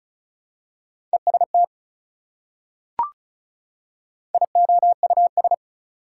Morse code, heard as a keyed tone, about 36 wpm, 700 Hz. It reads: EST IOUS
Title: Prefixes and Suffixes 35wpm